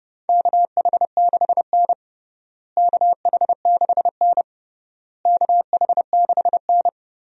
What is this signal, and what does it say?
Morse code, heard as a keyed tone, about 32 wpm, 700 Hz. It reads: KH6N KH6N KH6N
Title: DXCC-100 Call Signs - 4 Characters - Repeated 3x - 30wpm